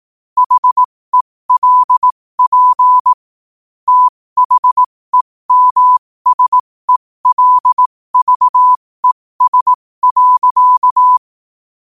Morse code, heard as a keyed tone, about 18 wpm, spaced as for 15 wpm, 1000 Hz. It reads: HELP THEMSELVES.
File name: qotdfiles / 15wpm